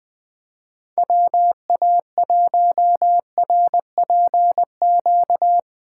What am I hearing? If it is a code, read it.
Morse code, heard as a keyed tone, about 20 wpm, 700 Hz. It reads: WA1RPQ